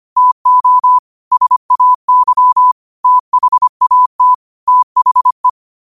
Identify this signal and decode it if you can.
Morse code, heard as a keyed tone, about 25 wpm, 1000 Hz. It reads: TO SAY THAT THE
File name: 25wpm